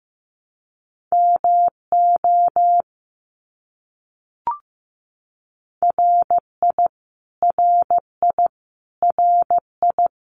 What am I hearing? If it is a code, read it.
Morse code, heard as a keyed tone, about 14 wpm, 700 Hz. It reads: MO RI RI RI
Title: US States Repeated 3x 15wpm